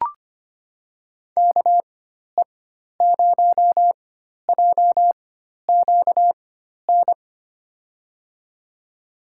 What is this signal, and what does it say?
Morse code, heard as a keyed tone, about 25 wpm, spaced as for 12 wpm, 700 Hz. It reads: KE0JQN
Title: USA Call Signs - 6 Characters - 4x Character Spacing - 25wpm